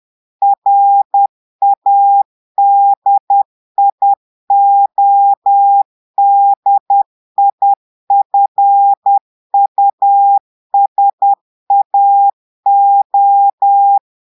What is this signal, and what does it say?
Morse code, heard as a keyed tone, about 10 wpm, 800 Hz. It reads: RADIODIFUSAO
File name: Portugais - texte en clair 1 - vitesse - 10 mots minute